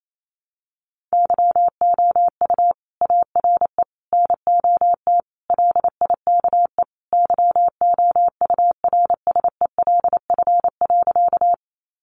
Morse code, heard as a keyed tone, about 28 wpm, 700 Hz. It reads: YOU ARE NOT LIKE YOURSELF.
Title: Sentences from Top 600 Words 28wpm